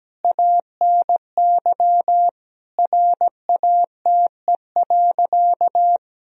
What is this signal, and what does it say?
Morse code, heard as a keyed tone, about 17 wpm, 700 Hz. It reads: ANY RATE.
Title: Sentences from Top 1000 Words - Review Part 4 - 17wpm